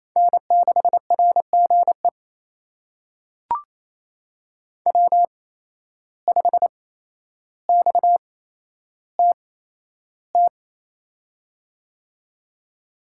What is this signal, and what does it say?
Morse code, heard as a keyed tone, about 28 wpm, spaced as for 12 wpm, 700 Hz. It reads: N6RGE W5XTT
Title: USA Call Signs - 5 Characters - 8x Character Spacing - 28wpm